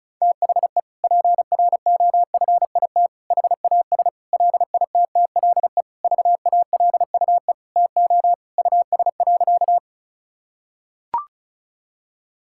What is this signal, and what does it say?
Morse code, heard as a keyed tone, about 34 wpm, 700 Hz. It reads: THE PROFIT HAS LITTLE VALUE TO US.
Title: Sentences from Top 1000 Words - Review Part 5 - 35wpm